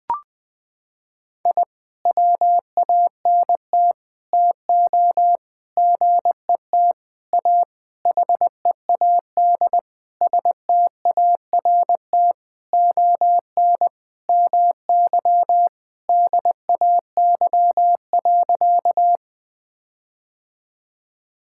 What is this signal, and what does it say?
Morse code, heard as a keyed tone, about 21 wpm, 700 Hz. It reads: I WANT TO GET A HEAD START ON MY DAY.
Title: Sentences with 9-Word Phrases - 20wpm